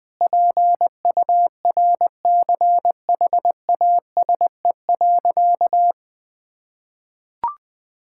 Morse code, heard as a keyed tone, about 20 wpm, 700 Hz. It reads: PURCHASE.
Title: Sentences from Top 1000 Words - Review Part 5 - 20wpm